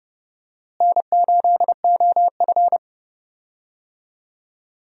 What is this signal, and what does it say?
Morse code, heard as a keyed tone, about 31 wpm, 700 Hz. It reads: N8OF
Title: USA Call Signs - 4 Characters - Repeated 3x - 30wpm